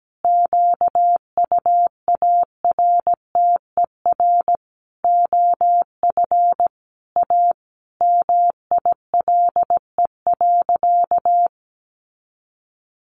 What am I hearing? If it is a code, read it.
Morse code, heard as a keyed tone, about 17 wpm, 700 Hz. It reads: QUARTER OF A MILE.
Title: Sentences from Top 1300 Words 17wpm